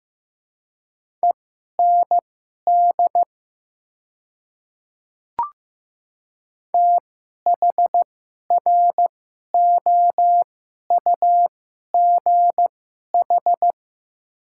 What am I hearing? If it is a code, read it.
Morse code, heard as a keyed tone, about 15 wpm, spaced as for 11 wpm, 700 Hz. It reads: END THROUGH
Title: 2x Character Spacing - Top 200 Words - 15wpm